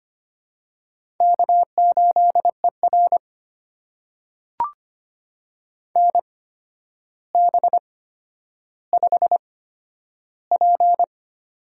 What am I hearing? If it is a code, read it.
Morse code, heard as a keyed tone, about 25 wpm, spaced as for 11 wpm, 700 Hz. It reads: K8ER NB5P
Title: USA Call Signs - 4 Characters - 8x Character Spacing - 25wpm